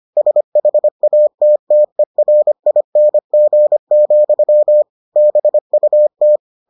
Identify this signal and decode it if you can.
Morse code, heard as a keyed tone, about 25 wpm, 600 Hz. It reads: SHATTERING, BUT